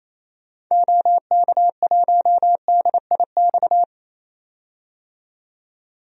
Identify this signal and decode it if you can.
Morse code, heard as a keyed tone, about 28 wpm, 700 Hz. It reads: OK1DIX
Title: DXCC-100 Call Signs - 6 Characters - 28wpm